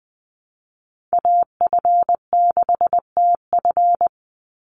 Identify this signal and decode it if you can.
Morse code, heard as a keyed tone, about 20 wpm, 700 Hz. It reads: AF6TF
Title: USA Call Signs 20wpm